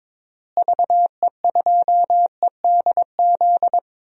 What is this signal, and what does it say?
Morse code, heard as a keyed tone, about 22 wpm, 700 Hz. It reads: VE2EDZ
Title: DXCC-100 Call Signs - 6 Characters - Repeated 3x - 22wpm